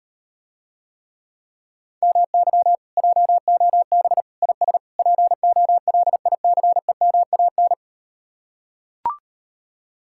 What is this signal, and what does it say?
Morse code, heard as a keyed tone, about 39 wpm, 700 Hz. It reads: MY JOB IS POLICEMAN